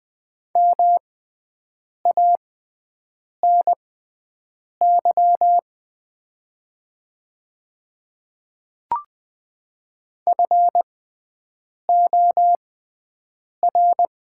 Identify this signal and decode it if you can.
Morse code, heard as a keyed tone, about 20 wpm, spaced as for 7 wpm, 700 Hz. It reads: MANY FOR
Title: 6x Character Spacing - Top 100 Words - 20wpm